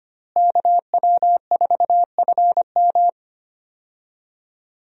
Morse code, heard as a keyed tone, about 25 wpm, 700 Hz. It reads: KW4FM
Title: USA Call Signs - 5 Characters - Repeated 3x - 25wpm